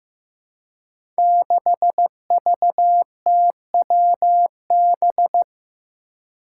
Morse code, heard as a keyed tone, about 15 wpm, 700 Hz. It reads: 6VTWB
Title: Five Challenging Letter-Number Combinations 15wpm